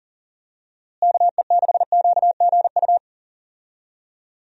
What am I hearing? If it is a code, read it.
Morse code, heard as a keyed tone, about 40 wpm, 700 Hz. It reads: KE6QGU
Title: USA Call Signs - 6 Characters - Repeated 3x - 40wpm